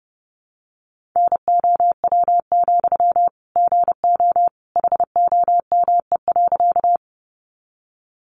Morse code, heard as a keyed tone, about 30 wpm, 700 Hz. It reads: NOW, GO HOME.
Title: Sentences from Top 200 Words 30wpm